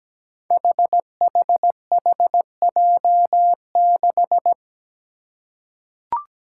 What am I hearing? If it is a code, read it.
Morse code, heard as a keyed tone, about 17 wpm, 700 Hz. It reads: HHHJ6
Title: Five Challenging Letter-Number Combinations 17wpm